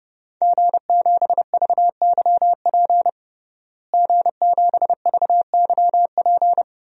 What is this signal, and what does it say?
Morse code, heard as a keyed tone, about 32 wpm, 700 Hz. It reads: G7VYP G7VYP
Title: DXCC-100 Call Signs - 5 Characters - Repeated 2x - 30wpm